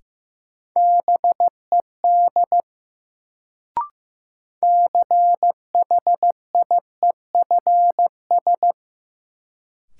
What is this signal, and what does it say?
Morse code, heard as a keyed tone, about 15 wpm, 700 Hz. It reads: BED CHIEFS
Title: Sets of 5 Words Encore - New Word Drill - 15wpm